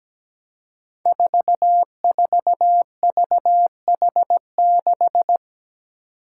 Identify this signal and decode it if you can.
Morse code, heard as a keyed tone, about 17 wpm, 700 Hz. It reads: 44VH6
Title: Five Challenging Letter-Number Combinations 17wpm